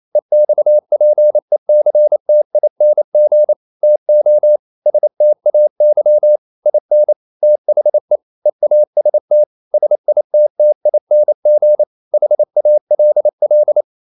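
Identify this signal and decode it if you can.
Morse code, heard as a keyed tone, about 28 wpm, 600 Hz. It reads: EXPECTING TO STAY IN THE EAST SITTING HALL
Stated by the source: Morse code practice at 28 wpm, text of Ken Follett's novel Never in English, from CW med Gnister.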